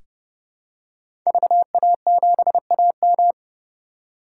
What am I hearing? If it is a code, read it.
Morse code, heard as a keyed tone, about 30 wpm, 700 Hz. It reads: VA7AM
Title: K1USN SST QSO Practice - 30wpm